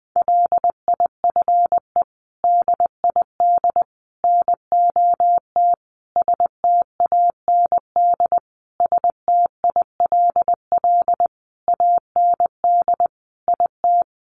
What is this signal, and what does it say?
Morse code, heard as a keyed tone, about 20 wpm, 700 Hz. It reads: LIFE DID NOT STAND STILL AND IT
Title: Sentences from Top 800 Words 20wpm